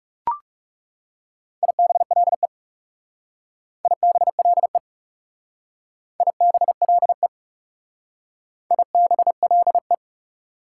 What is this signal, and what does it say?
Morse code, heard as a keyed tone, about 37 wpm, 690 Hz. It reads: IBLE IBLE IBLE IBLE
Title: Speed-Racing Prefixes and Suffixes 30wpm